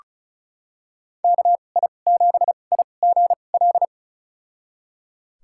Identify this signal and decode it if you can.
Morse code, heard as a keyed tone, about 36 wpm, spaced as for 25 wpm, 700 Hz. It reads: KI7IGL
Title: USA Call Signs - 6 Characters - 2x Character Spacing - 35wpm